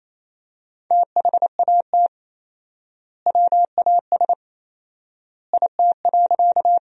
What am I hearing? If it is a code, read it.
Morse code, heard as a keyed tone, about 28 wpm, 700 Hz. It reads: THAT WAS IT.
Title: 4x Word-Spacing - Sentences from Top 100 Words - 28wpm